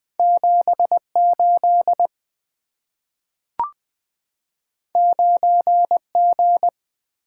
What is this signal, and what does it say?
Morse code, heard as a keyed tone, about 20 wpm, 700 Hz. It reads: 78 9G
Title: Two Letter-Number Combinations - 20wpm